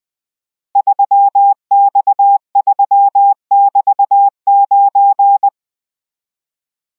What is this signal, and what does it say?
Morse code, heard as a keyed tone, about 20 wpm, 800 Hz. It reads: 3X3<BT>9